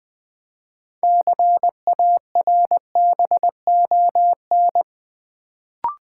Morse code, heard as a keyed tone, about 21 wpm, 700 Hz. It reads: CARBON
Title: Build A Word - 3 Letters At A Time - 6-Letter Words - 20wpm